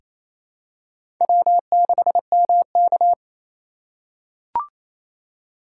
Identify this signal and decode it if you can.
Morse code, heard as a keyed tone, about 28 wpm, 700 Hz. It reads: W6MK
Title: USA Call Signs - 4 Characters - 4x Character Spacing - 28wpm